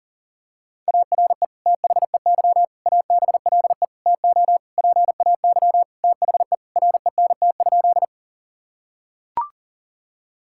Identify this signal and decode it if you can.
Morse code, heard as a keyed tone, about 40 wpm, 700 Hz. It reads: ARE THEY ABLE TO PAY THE RENT?
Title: Sentences from Top 1000 Words - Review Part 5 - 40wpm